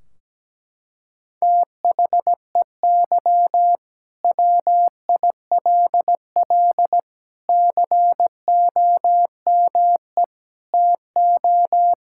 Morse code, heard as a keyed tone, about 17 wpm, 700 Hz. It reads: THEY WILL COME TO
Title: Sentences from Top 1600 Words 17wpm